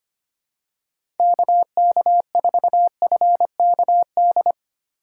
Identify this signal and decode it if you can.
Morse code, heard as a keyed tone, about 25 wpm, 700 Hz. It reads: KK4FKD
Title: USA Call Signs - 6 Characters - 4x Character Spacing - 25wpm